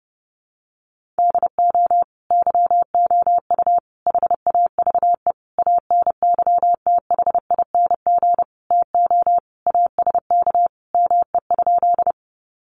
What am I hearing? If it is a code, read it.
Morse code, heard as a keyed tone, about 28 wpm, 700 Hz. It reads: DO YOU HAVE ANYTHING TO ASK ME?
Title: Sentences from Top 1000 Words 30wpm